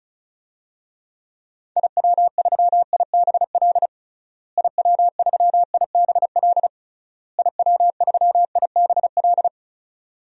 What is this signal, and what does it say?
Morse code, heard as a keyed tone, about 36 wpm, 700 Hz. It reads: IW3IBL IW3IBL IW3IBL
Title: DXCC-100 Call Signs - 6 Characters - Repeated 3x - 35wpm